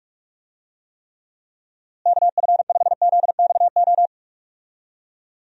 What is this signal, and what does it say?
Morse code, heard as a keyed tone, about 46 wpm, 700 Hz. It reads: KF5ZXO
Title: USA Call Signs - 6 Characters - Repeated 3x - 45wpm